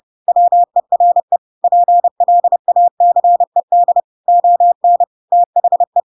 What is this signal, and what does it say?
Morse code, heard as a keyed tone, about 30 wpm, 700 Hz. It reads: WERE PLACED ON THE
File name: Pinocchio / Pinocchio ch7 30 30 ik1hge